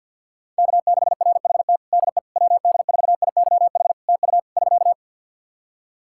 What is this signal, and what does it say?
Morse code, heard as a keyed tone, about 49 wpm, 700 Hz. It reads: K6RHT DE WD4IYH TU <SK>
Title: Practice QSO - Exchange 4 - Ending - 50wpm